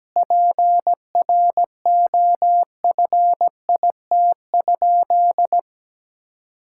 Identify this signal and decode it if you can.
Morse code, heard as a keyed tone, about 17 wpm, 700 Hz. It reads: PROFIT?
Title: Sentences from Top 1000 Words - Review Part 5 - 17wpm